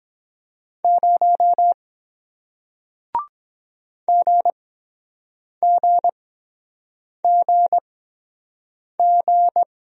Morse code, heard as a keyed tone, about 23 wpm, spaced as for 7 wpm, 700 Hz. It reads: 0 GGGG